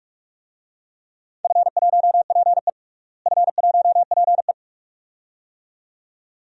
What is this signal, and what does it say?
Morse code, heard as a keyed tone, about 46 wpm, 700 Hz. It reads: F1PE F1PE
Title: DXCC-100 Call Signs - 4 Characters - Repeated 2x - 45wpm